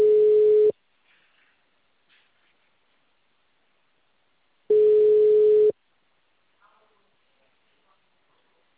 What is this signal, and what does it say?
Ringback tone (Europe, 425 Hz).
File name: out- -2030- - - .211